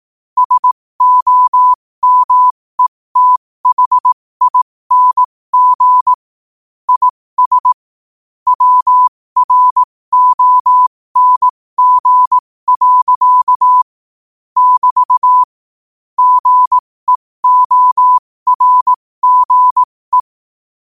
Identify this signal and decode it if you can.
Morse code, heard as a keyed tone, about 18 wpm, spaced as for 15 wpm, 1000 Hz. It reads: SOMETHING IS WRONG. <BT> GEORGE
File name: qotdfiles / 15wpm